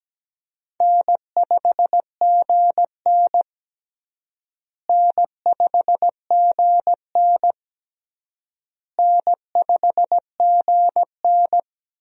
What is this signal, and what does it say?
Morse code, heard as a keyed tone, about 17 wpm, 700 Hz. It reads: N5GN N5GN N5GN
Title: USA Call Signs - 4 Characters - Repeated 3x - 17wpm